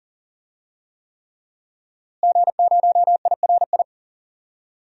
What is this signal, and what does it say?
Morse code, heard as a keyed tone, about 40 wpm, 700 Hz. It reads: G0IRI
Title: DXCC-100 Call Signs - 5 Characters - Repeated 3x - 40wpm